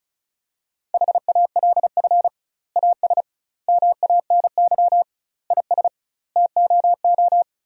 Morse code, heard as a keyed tone, about 36 wpm, 700 Hz. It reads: HALF AS MANY IS TOO
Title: 2x Word-Spacing - Sentences from Top 400 Words - 35wpm